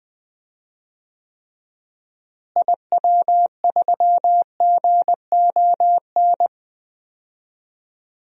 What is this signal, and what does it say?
Morse code, heard as a keyed tone, about 20 wpm, 700 Hz. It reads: IW3GON
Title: DXCC-100 Call Signs - 6 Characters - Repeated 3x - 20wpm